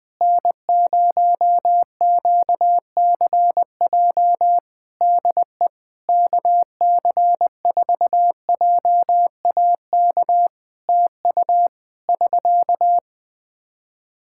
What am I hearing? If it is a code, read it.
Morse code, heard as a keyed tone, about 20 wpm, 700 Hz. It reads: N0QCJ DE KC4JAK TU <SK>
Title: K1USN SST QSO Practice 20wpm